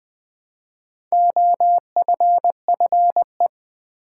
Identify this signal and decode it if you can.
Morse code, heard as a keyed tone, about 20 wpm, 700 Hz. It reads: OFFE
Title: Build A Word - 1 Letter At A Time - 3 to 5-Letter Words - 20wpm